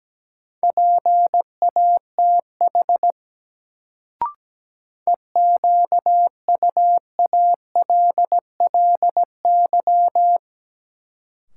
Morse code, heard as a keyed tone, about 17 wpm, 700 Hz. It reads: PATH EQUALLY
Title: Sentences from Top 1400 Words - New Word Drill - 17wpm